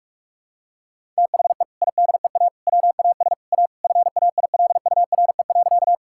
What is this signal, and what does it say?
Morse code, heard as a keyed tone, about 43 wpm, 700 Hz. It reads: THE IDEA WAS A FAILURE.
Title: Sentences from Top 1000 Words - Review Part 6 - 45wpm